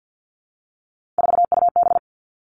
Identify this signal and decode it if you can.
Morse code, heard as a keyed tone, about 50 wpm, 700 Hz. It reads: 4UB
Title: Three Challenging Letter-Number Combinations 50wpm